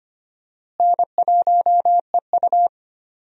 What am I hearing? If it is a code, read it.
Morse code, heard as a keyed tone, about 25 wpm, 700 Hz. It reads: N1EU
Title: USA Call Signs - 4 Characters - Repeated 2x - 25wpm